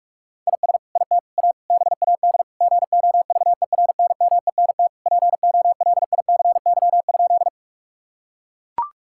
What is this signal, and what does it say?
Morse code, heard as a keyed tone, about 41 wpm, 700 Hz. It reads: IS IT A BAD GOVERNMENT POLICY?
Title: Sentences from Top 1000 Words - Review Part 5 - 45wpm